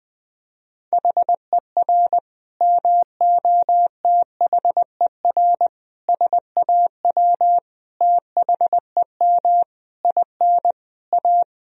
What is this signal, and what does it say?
Morse code, heard as a keyed tone, about 21 wpm, 700 Hz. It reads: HER MOTHER SAW THEM IN A
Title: Sentences from Top 1000 Words - Review Part 3 - 20wpm